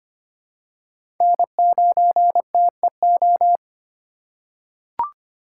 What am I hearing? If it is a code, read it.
Morse code, heard as a keyed tone, about 25 wpm, 700 Hz. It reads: N9TEO
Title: USA Call Signs - 4x Character Spacing - 25wpm